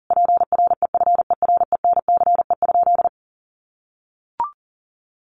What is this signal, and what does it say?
Morse code, heard as a keyed tone, about 40 wpm, 700 Hz. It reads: PREFERENCE?
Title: Sentences from Top 1500 Words 40wpm